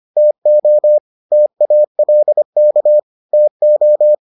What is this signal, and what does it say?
Morse code, heard as a keyed tone, about 25 wpm, 600 Hz. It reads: TO TALK TO